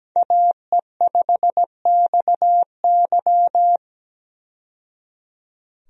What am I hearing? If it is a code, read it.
Morse code, heard as a keyed tone, about 17 wpm, 700 Hz. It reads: AE5XY
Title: USA Call Signs - 5 Characters - 17wpm